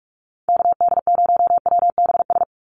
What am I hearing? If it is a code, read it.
Morse code, heard as a keyed tone, about 46 wpm, 690 Hz. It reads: KD0WBS
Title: USA Call Signs 45wpm